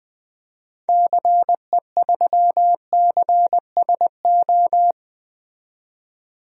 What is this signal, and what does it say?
Morse code, heard as a keyed tone, about 20 wpm, 700 Hz. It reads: CE3CSO
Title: DXCC-100 Call Signs - 6 Characters - 20wpm